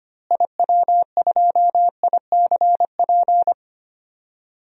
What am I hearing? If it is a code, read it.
Morse code, heard as a keyed tone, about 25 wpm, 700 Hz. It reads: IW2ICP
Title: DXCC-100 Call Signs - 6 Characters - Repeated 2x - 25wpm